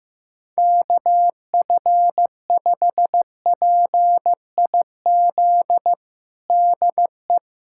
Morse code, heard as a keyed tone, about 15 wpm, 700 Hz. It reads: KF5PIZ DE
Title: USA Callsign DE Callsign 15wpm